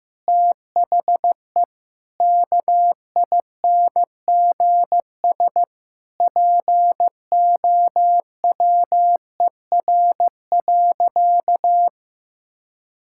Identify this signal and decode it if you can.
Morse code, heard as a keyed tone, about 15 wpm, 700 Hz. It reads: THE KINGS POWER.
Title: Sentences from Top 1000 Words - Review Part 6 - 15wpm